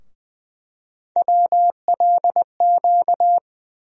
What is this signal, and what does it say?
Morse code, heard as a keyed tone, about 20 wpm, 700 Hz. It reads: WLQ
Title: USA Call Sign Postfixes - 3 Letters 20wpm